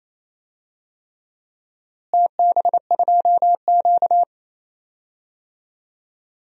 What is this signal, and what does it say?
Morse code, heard as a keyed tone, about 28 wpm, 700 Hz. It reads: TB2Q
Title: DXCC-100 Call Signs - 4 Characters - Repeated 3x - 28wpm